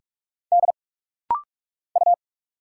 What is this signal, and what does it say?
Morse code, heard as a keyed tone, about 46 wpm, 700 Hz. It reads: D U
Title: Single Letter-Number Alt - Rapid-Fire 45wpm